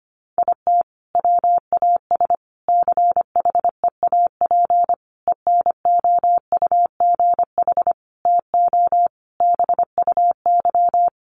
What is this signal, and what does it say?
Morse code, heard as a keyed tone, about 25 wpm, 700 Hz. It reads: IT WAS CHEAP ENOUGH TO BUY
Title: Sentences from Top 1200 Words 25wpm